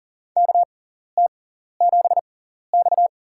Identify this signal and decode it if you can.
Morse code, heard as a keyed tone, about 37 wpm, spaced as for 13 wpm, 700 Hz. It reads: KT7X
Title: USA Call Signs - 4 Characters - 6x Character Spacing - 40wpm